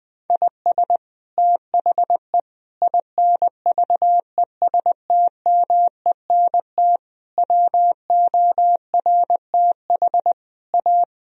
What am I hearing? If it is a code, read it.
Morse code, heard as a keyed tone, about 20 wpm, 700 Hz. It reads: IS THE INVESTMENT WORTH A
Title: Sentences from Top 1000 Words - Review Part 4 - 20wpm